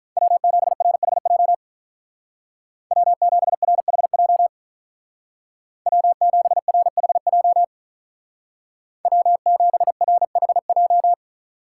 Morse code, heard as a keyed tone, about 43 wpm, 700 Hz. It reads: W7RHJ W7RHJ W7RHJ W7RHJ
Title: Speed Racing - USA 1x3 Call Signs 35wpm